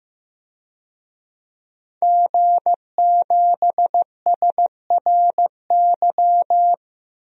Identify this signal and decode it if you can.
Morse code, heard as a keyed tone, about 15 wpm, 700 Hz. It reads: G7SRY
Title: DXCC-100 Call Signs - 5 Characters - Repeated 2x - 15wpm